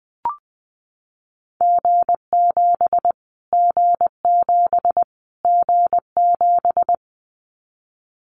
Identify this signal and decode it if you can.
Morse code, heard as a keyed tone, about 20 wpm, 700 Hz. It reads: G7 G7 G7
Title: Callsign Prefixes for Top 25 DXCC - Repeated 3x - 20wpm